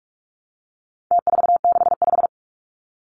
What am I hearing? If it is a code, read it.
Morse code, heard as a keyed tone, about 44 wpm, 700 Hz. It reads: T465
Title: Numbers - Contest Style - 4-Digit 45wpm